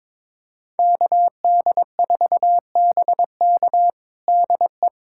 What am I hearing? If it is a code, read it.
Morse code, heard as a keyed tone, about 22 wpm, 700 Hz. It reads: KD4BK DE
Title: USA Callsign DE Callsign 22wpm